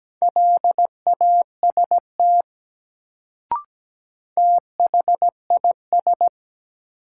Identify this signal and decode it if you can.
Morse code, heard as a keyed tone, about 17 wpm, 700 Hz. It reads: LAST THIS
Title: Sets of 2 Words - New Word Drill - 17wpm